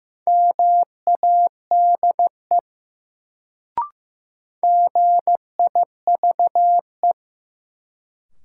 Morse code, heard as a keyed tone, about 15 wpm, 700 Hz. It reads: MADE GIVE
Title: Sentences from Top 200 Words - New Word Drill - 15wpm